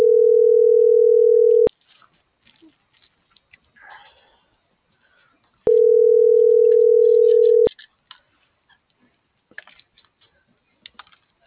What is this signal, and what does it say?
Ringback tone (North America), 440 + 480 Hz.